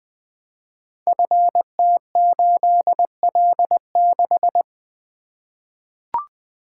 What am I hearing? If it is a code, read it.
Morse code, heard as a keyed tone, about 20 wpm, 700 Hz. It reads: FT8L6
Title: Code Groups - 5 Letter:Numbers - 20wpm